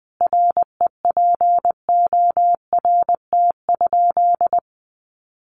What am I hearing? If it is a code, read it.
Morse code, heard as a keyed tone, about 20 wpm, 700 Hz. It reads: REPORT?
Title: Sentences from Top 900 Words 20wpm